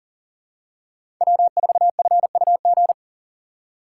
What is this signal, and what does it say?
Morse code, heard as a keyed tone, about 37 wpm, 700 Hz. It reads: W4FUG
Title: USA Call Signs - 5 Characters - 2x Character Spacing - 40wpm